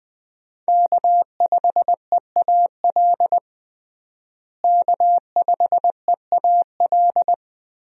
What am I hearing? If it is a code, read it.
Morse code, heard as a keyed tone, about 20 wpm, 700 Hz. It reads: K5EAL K5EAL
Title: USA Call Signs - 5 Characters - Repeated 3x - 20wpm